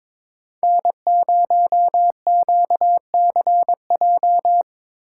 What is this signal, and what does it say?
Morse code, heard as a keyed tone, about 22 wpm, 700 Hz. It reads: N0QCJ
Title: USA Call Signs - 5 Characters - 2x Character Spacing - 22wpm